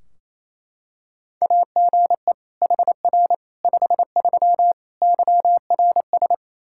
Morse code, heard as a keyed tone, about 28 wpm, 700 Hz. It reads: AGE HR 53 YRS
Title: Practice QSO - Exchange 3 - More Info - 28wpm